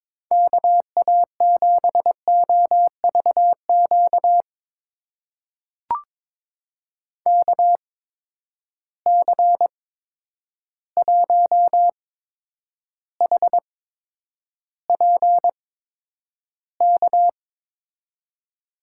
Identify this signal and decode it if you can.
Morse code, heard as a keyed tone, about 22 wpm, spaced as for 9 wpm, 700 Hz. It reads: KA7OVQ KC1HPK
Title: USA Call Signs - 6 Characters - 8x Character Spacing - 22wpm